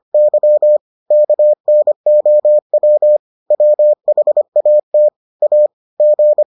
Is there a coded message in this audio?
Yes: Morse code from 0.1 to 6.4 s, heard as a keyed tone, about 25 wpm, 600 Hz. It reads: Y KNOW WHAT A G